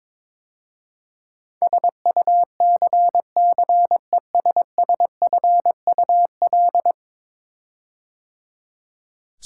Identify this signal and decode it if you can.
Morse code, heard as a keyed tone, about 22 wpm, 700 Hz. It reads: SUCCESSFUL
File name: practice / 22wpm / 10 Letter Words - First 3 Letters Given - 22wpm